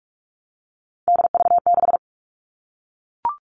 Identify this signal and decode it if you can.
Morse code, heard as a keyed tone, about 44 wpm, 700 Hz. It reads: DV6
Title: All Cut Numbers - 3-Digit 45wpm